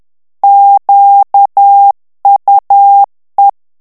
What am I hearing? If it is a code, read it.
Morse code, heard as a keyed tone, about 10 wpm, 800 Hz. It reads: QUE
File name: CW / FICHIERS-MP3 / FRANCAIS TEXTE CLAIR 2 VITESSE 10 MOTS MINUTES